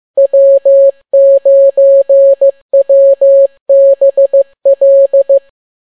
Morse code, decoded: W9WBL